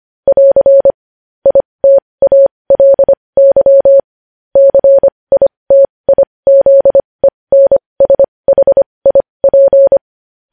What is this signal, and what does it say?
Morse code, heard as a keyed tone, about 25 wpm, spaced as for 20 wpm, 550 Hz. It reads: <AR> ITALY CITIZENSHIP
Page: News Headlines in Morse Code at 20 WPM